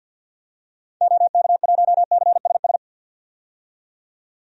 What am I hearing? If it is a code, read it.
Morse code, heard as a keyed tone, about 52 wpm, 700 Hz. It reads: OK1CSS
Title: DXCC-100 Call Signs - 6 Characters - 50wpm